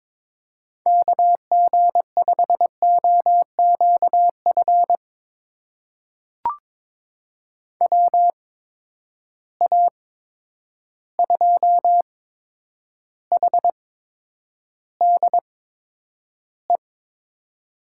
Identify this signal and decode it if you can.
Morse code, heard as a keyed tone, about 22 wpm, spaced as for 9 wpm, 700 Hz. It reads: KG5OQF WA2HDE
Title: USA Call Signs - 6 Characters - 8x Character Spacing - 22wpm